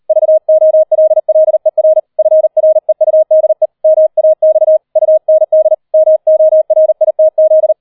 Morse code, heard as a keyed tone, about 38 wpm, 630 Hz. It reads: VOLLER FREUDE MAX UND MORITZ